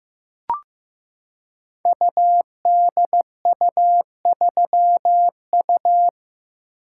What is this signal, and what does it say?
Morse code, heard as a keyed tone, about 15 wpm, 700 Hz. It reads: UDU3U
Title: Code Groups - 5 Letter:Numbers - 15wpm